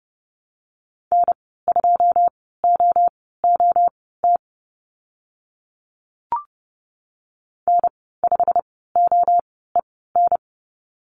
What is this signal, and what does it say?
Morse code, heard as a keyed tone, about 30 wpm, spaced as for 17 wpm, 700 Hz. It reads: N2OOT N5OEN
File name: Lesson 15g - Number 5 - Call Signs 30wpm